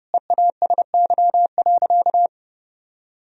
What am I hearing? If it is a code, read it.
Morse code, heard as a keyed tone, about 30 wpm, 700 Hz. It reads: EASY.
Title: Sentences from Top 1000 Words - Review Part 2 - 30wpm